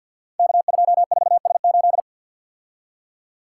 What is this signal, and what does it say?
Morse code, heard as a keyed tone, about 52 wpm, 700 Hz. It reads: K24S8
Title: Code Groups - 5 Letter:Numbers - 50wpm